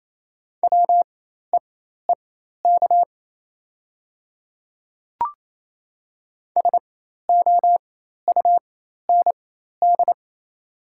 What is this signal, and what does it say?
Morse code, heard as a keyed tone, about 28 wpm, spaced as for 13 wpm, 700 Hz. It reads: WEEK SOUND
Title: 4x Character Spacing - Top 500 Words - 28wpm